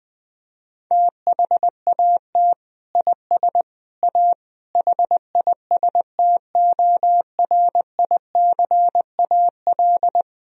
Morse code, heard as a keyed tone, about 20 wpm, 700 Hz. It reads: THAT IS A HISTORICAL
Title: Sentences from Top 1000 Words - Review Part 4 - 20wpm